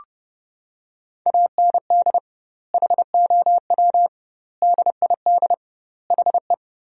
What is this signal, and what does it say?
Morse code, heard as a keyed tone, about 30 wpm, 700 Hz. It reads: AND HOW DID HE
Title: 2x Word-Spacing - Sentences from Top 400 Words - 30wpm